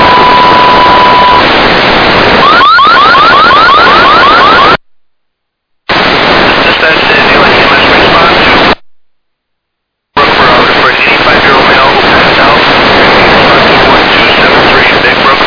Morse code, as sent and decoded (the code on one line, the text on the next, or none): none